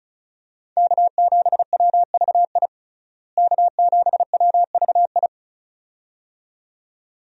Morse code, heard as a keyed tone, about 34 wpm, 700 Hz. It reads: K7WVI K7WVI
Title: USA Call Signs - 5 Characters - Repeated 2x - 35wpm